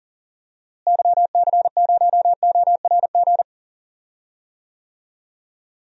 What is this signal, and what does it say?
Morse code, heard as a keyed tone, about 40 wpm, 700 Hz. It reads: YC0ORG
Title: DXCC-100 Call Signs - 6 Characters - Repeated 2x - 40wpm